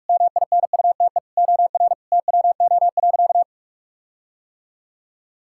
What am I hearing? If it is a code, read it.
Morse code, heard as a keyed tone, about 46 wpm, 700 Hz. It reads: MINUTE OR TWO.
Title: Sentences from Top 1600 Words 45wpm